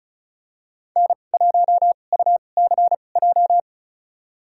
Morse code, heard as a keyed tone, about 36 wpm, spaced as for 25 wpm, 700 Hz. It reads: N1UCJ
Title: USA Call Signs - 5 Characters - 2x Character Spacing - 35wpm